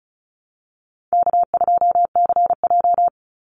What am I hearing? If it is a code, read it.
Morse code, heard as a keyed tone, about 34 wpm, 700 Hz. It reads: K2CJ
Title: Easy Call Signs Repeated 3x - 35wpm